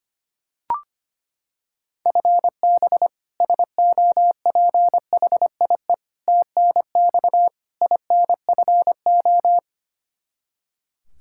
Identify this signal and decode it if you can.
Morse code, heard as a keyed tone, about 25 wpm, 700 Hz. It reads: FB SOPHIE TNX INFO
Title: K1USN SST QSO Practice 25wpm